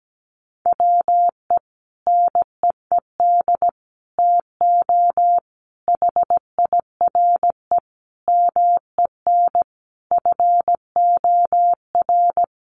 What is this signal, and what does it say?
Morse code, heard as a keyed tone, about 17 wpm, 700 Hz. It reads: WE NEED TO HIRE MEN FOR
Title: Sentences from Top 1400 Words 17wpm